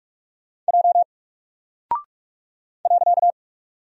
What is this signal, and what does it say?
Morse code, heard as a keyed tone, about 45 wpm, 700 Hz. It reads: J .
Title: Single Character (Letter:Number:Punctuation:Prosign) 45wpm